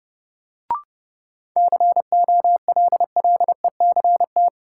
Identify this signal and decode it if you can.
Morse code, heard as a keyed tone, about 31 wpm, 700 Hz. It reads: COLLECT
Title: Sentences from Top 1600 Words - New Word Drill - 30wpm